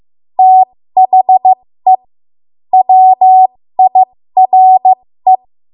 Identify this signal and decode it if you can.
Morse code, heard as a keyed tone, about 15 wpm, spaced as for 13 wpm, 750 Hz. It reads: THE WIRE
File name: Morse / 13WPM